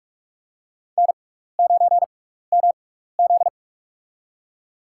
Morse code, heard as a keyed tone, about 43 wpm, spaced as for 15 wpm, 700 Hz. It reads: N9MZ
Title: USA Call Signs - 4 Characters - 6x Character Spacing - 45wpm